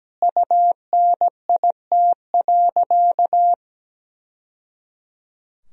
Morse code, heard as a keyed tone, about 17 wpm, 700 Hz. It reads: UNIT.
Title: Sentences from Top 1000 Words - Review Part 2 - 17wpm